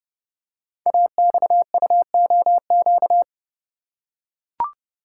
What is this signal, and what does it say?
Morse code, heard as a keyed tone, about 29 wpm, 700 Hz. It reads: AXUOQ
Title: Code Groups - 5 Letters - 30wpm